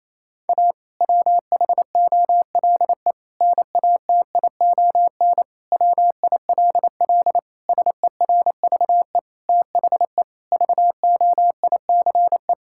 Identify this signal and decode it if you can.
Morse code, heard as a keyed tone, about 28 wpm, 700 Hz. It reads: A WHOLE NATION WILL SERVE THE VOICE
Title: Sentences from Top 1000 Words - Review Part 6 - 28wpm